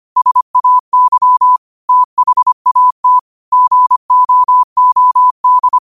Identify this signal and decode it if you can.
Morse code, heard as a keyed tone, about 25 wpm, 1000 Hz. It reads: SAY THAT GOOD
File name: qotdfiles / 25wpm